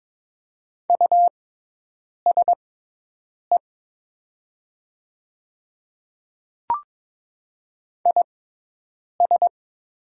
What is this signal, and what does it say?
Morse code, heard as a keyed tone, about 22 wpm, spaced as for 8 wpm, 700 Hz. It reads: USE IS
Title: 6x Character Spacing - Top 500 Words - 22wpm